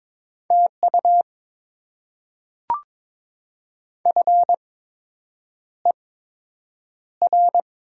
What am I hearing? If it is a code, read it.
Morse code, heard as a keyed tone, about 22 wpm, spaced as for 8 wpm, 700 Hz. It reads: TU FER
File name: Top 20 QSO Elements - 8x Character Spacing - 22wpm